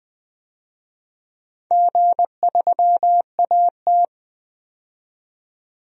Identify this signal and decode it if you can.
Morse code, heard as a keyed tone, about 20 wpm, 700 Hz. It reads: G3AT
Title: DXCC-100 Call Signs - 4 Characters - 20wpm